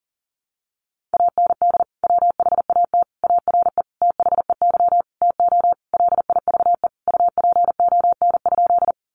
Morse code, heard as a keyed tone, about 40 wpm, 700 Hz. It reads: AND WHAT ARE THEY TO LIVE UPON?
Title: Sentences from Top 1100 Words 40wpm